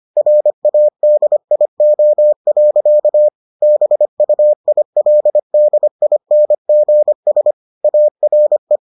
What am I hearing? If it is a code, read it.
Morse code, heard as a keyed tone, about 25 wpm, 600 Hz. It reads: RADIO. BUILDINGS ARE